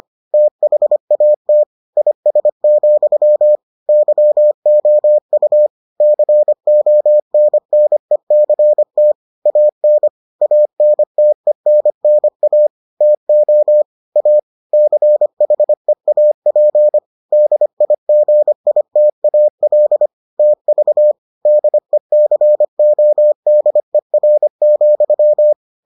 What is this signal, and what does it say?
Morse code, heard as a keyed tone, about 25 wpm, 600 Hz. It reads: THAT IS, YOU CONNECT AN ANTENNA TO A CHEAP DIGITAL TV DECODER,